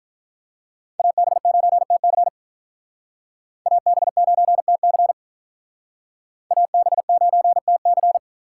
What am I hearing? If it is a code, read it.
Morse code, heard as a keyed tone, about 47 wpm, 700 Hz. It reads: AB9TC AB9TC AB9TC